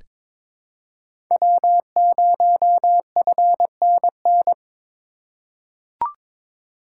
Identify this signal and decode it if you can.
Morse code, heard as a keyed tone, about 22 wpm, 700 Hz. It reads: W0FNN
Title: USA Call Signs - 5 Characters - 4x Character Spacing - 22wpm